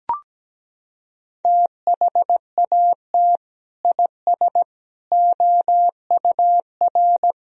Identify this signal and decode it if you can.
Morse code, heard as a keyed tone, about 17 wpm, 700 Hz. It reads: THAT IS OUR
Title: Sentences from Top 1000 Words - Review Part 2 - 17wpm